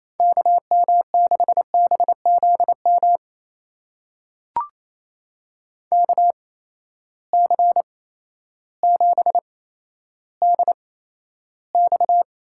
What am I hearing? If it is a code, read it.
Morse code, heard as a keyed tone, about 28 wpm, spaced as for 13 wpm, 700 Hz. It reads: KM6BZM KC7DX
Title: USA Call Signs - 6 Characters - 8x Character Spacing - 28wpm